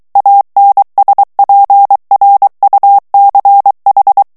Morse code, heard as a keyed tone, about 23 wpm, 800 Hz. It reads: ANSPRUCH
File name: morse mp3 / ALLEMAND texte clair 1 vitesse 22 mots minute